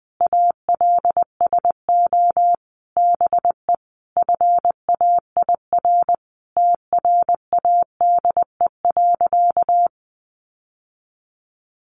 Morse code, heard as a keyed tone, about 20 wpm, 700 Hz. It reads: ALSO BE FAIR TRADE.
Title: Sentences from Top 1300 Words 20wpm